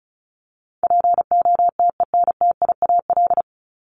Morse code, heard as a keyed tone, about 34 wpm, 700 Hz. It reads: POTENTIAL
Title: Top 501-750 Words 35wpm